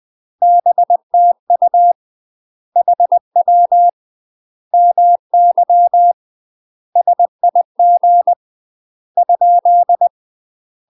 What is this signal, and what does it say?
Morse code, heard as a keyed tone, about 20 wpm, 700 Hz. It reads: BTU HW MY SIG ?
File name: UK MASTER QSOs TRAINING / 20WPM / UK MASTER QSOs 0029